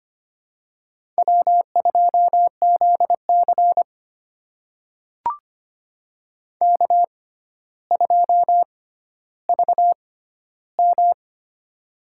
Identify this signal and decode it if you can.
Morse code, heard as a keyed tone, about 25 wpm, spaced as for 13 wpm, 700 Hz. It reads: W2ZC K2VM
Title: USA Call Signs - 4 Characters - 6x Character Spacing - 25wpm